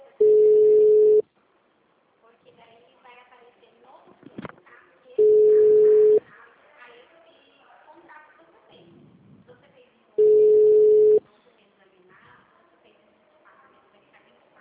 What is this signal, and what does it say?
Ringback tone (Europe, 425 Hz), with speech.